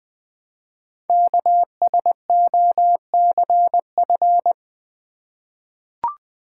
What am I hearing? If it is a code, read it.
Morse code, heard as a keyed tone, about 20 wpm, 700 Hz. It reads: KSOCF
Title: Code Groups - 5 Letter:Numbers - 20wpm